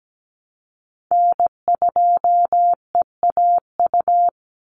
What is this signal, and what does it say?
Morse code, heard as a keyed tone, about 17 wpm, 700 Hz. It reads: N2EAU
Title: USA Call Signs 17wpm